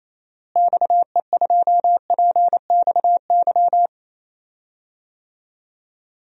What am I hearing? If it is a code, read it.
Morse code, heard as a keyed tone, about 28 wpm, 700 Hz. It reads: XE2PXY
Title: DXCC-100 Call Signs - 6 Characters - Repeated 3x - 28wpm